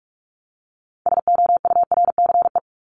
Morse code, heard as a keyed tone, about 44 wpm, 700 Hz. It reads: SOURCE
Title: Top 501-750 Words 45wpm